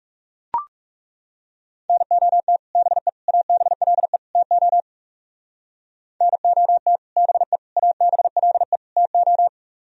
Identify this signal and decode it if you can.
Morse code, heard as a keyed tone, about 37 wpm, 700 Hz. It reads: NOT BE ABLE TO NOT BE ABLE TO
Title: Speed-Racing - Sets of 4 Words 30wpm